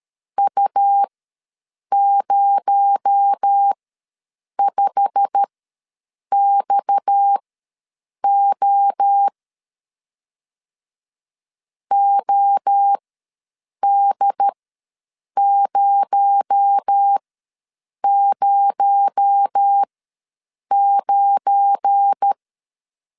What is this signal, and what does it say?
Morse code, heard as a keyed tone, about 13 wpm, spaced as for 7 wpm, 800 Hz. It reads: U05XO OD009
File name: LEZIONI / 8 YJ45 / 8 YJ45 ws 12wpm 3146